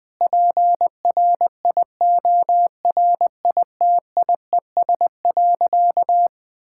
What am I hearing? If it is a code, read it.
Morse code, heard as a keyed tone, about 20 wpm, 700 Hz. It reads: PRIORITIES.
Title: Sentences with 7-Word Phrases - 20wpm